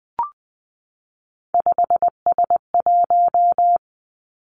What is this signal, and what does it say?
Morse code, heard as a keyed tone, about 20 wpm, 700 Hz. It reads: 5S1
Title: Three Challenging Letter-Number Combinations 20wpm